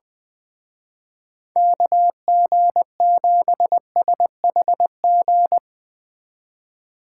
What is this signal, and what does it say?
Morse code, heard as a keyed tone, about 20 wpm, 700 Hz. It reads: KG7SHG
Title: Speed Racing - USA 2x3 Call Signs 17wpm